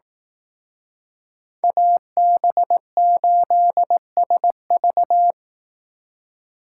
Morse code, heard as a keyed tone, about 18 wpm, 700 Hz. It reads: AB8SV
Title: Speed Racing - USA 2x2 Call Signs 15wpm